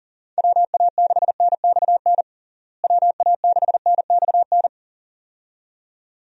Morse code, heard as a keyed tone, about 40 wpm, 700 Hz. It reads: WA6NXN WA6NXN
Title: USA Call Signs - 6 Characters - Repeated 2x - 40wpm